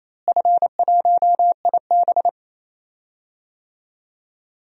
Morse code, heard as a keyed tone, about 28 wpm, 700 Hz. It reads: F1IB
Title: DXCC-100 Call Signs - 4 Characters - Repeated 3x - 28wpm